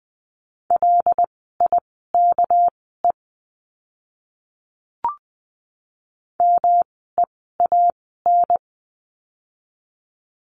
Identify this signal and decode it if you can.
Morse code, heard as a keyed tone, about 20 wpm, spaced as for 15 wpm, 700 Hz. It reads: LIKE MEAN
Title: 230 Lesson 37f - Letter x - Words 20wpm